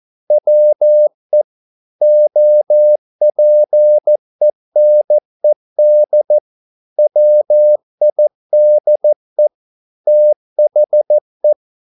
Morse code, heard as a keyed tone, about 14 wpm, 600 Hz. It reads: WE OPENED WIDE THE
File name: mp3 files / a dogs mistake - 14wpm